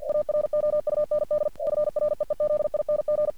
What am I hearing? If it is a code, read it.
Morse code, heard as a keyed tone, about 48 wpm, 600 Hz. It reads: AROUND FREEZING
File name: archived / CRSnet